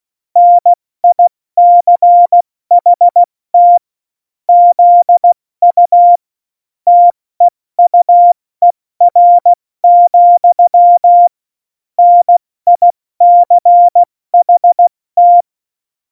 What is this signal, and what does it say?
Morse code, heard as a keyed tone, about 16 wpm, spaced as for 14 wpm, 700 Hz. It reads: NICHT ZU TEUER, NICHT
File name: Morsekurs / Witze / Witz 23. April 2019, 70 F80